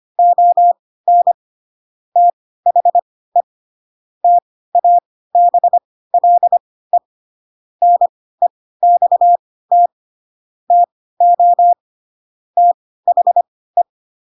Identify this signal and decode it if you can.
Morse code, heard as a keyed tone, about 25 wpm, spaced as for 16 wpm, 700 Hz. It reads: ON THE TABLE NEXT TO THE
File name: Pinocchio / Pinocchio ch7 10 25 ik1hge